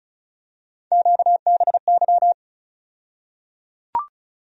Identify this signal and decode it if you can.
Morse code, heard as a keyed tone, about 36 wpm, 700 Hz. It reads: QBY
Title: USA Call Sign Postfixes - 3 Letters 35wpm